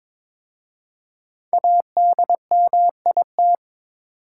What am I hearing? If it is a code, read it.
Morse code, heard as a keyed tone, about 22 wpm, 690 Hz. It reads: ADMIT
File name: 5 Letter Words - First 3 Letters Given - 22wpm